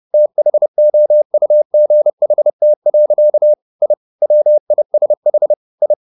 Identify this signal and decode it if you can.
Morse code, heard as a keyed tone, about 30 wpm, 600 Hz. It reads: THOUGHT. I WISH I
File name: the old man and the sea / the old man and the sea 1 30wpm